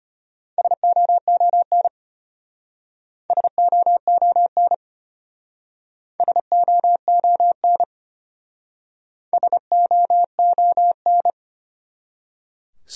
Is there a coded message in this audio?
Yes: Morse code from 0.6 to 11.3 s, heard as a keyed tone, about 33 wpm, 700 Hz. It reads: SOON SOON SOON SOON